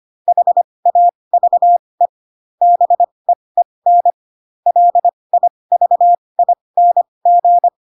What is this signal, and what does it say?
Morse code, heard as a keyed tone, about 25 wpm, spaced as for 20 wpm, 700 Hz. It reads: HAVE BEEN LIVING
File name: Pinocchio / Pinocchio ch4 15 25 ik1hge